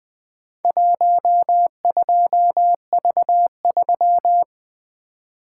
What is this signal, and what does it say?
Morse code, heard as a keyed tone, about 22 wpm, 700 Hz. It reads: 12V3
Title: Speed-Racing - Four Letter-Number Combinations - 20wpm